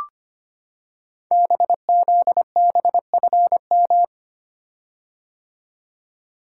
Morse code, heard as a keyed tone, about 25 wpm, 700 Hz. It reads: BZBFM
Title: Code Groups - 5 Letters - 25wpm